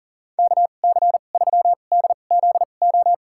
Morse code, heard as a keyed tone, about 40 wpm, spaced as for 29 wpm, 700 Hz. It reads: KC3DZO
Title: USA Call Signs - 2x Character Spacing - 40wpm